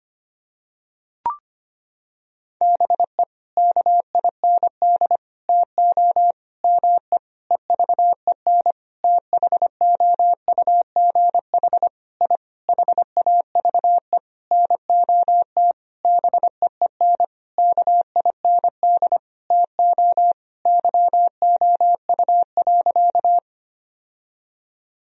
Morse code, heard as a keyed tone, about 25 wpm, 700 Hz. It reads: BE KIND TO ME EVEN THOUGH I HAVE NOT BEEN KIND TO YOU.
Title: Sentences from Top 1000 Words - Review Part 2 - 25wpm